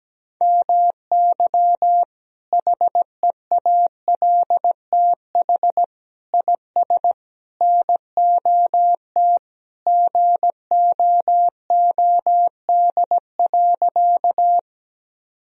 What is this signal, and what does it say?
Morse code, heard as a keyed tone, about 17 wpm, 700 Hz. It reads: MY HEALTH IS NOT GOOD.
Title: Sentences from Top 1000 Words - Review Part 3 - 17wpm